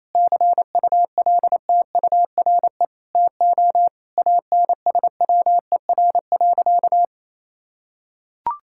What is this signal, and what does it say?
Morse code, heard as a keyed tone, about 28 wpm, 700 Hz. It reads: CULTURE TO ANSWER.
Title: Sentences from Top 1000 Words - Review Part 6 - 28wpm